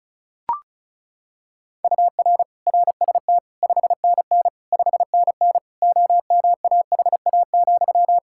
Morse code, heard as a keyed tone, about 34 wpm, 700 Hz. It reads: UR RST 5NN 5NN OMAHA,
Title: Practice QSO - Exchange 1 - Beginning - 35wpm